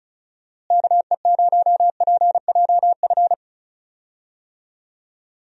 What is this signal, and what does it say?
Morse code, heard as a keyed tone, about 36 wpm, 700 Hz. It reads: KE0PJF
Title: USA Call Signs - 6 Characters - Repeated 3x - 35wpm